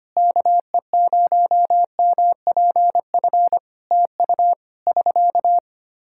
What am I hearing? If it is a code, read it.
Morse code, heard as a keyed tone, about 25 wpm, 700 Hz. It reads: KE0MPF TU <SK>
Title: Practice QSO - Exchange 4 - Ending - 25wpm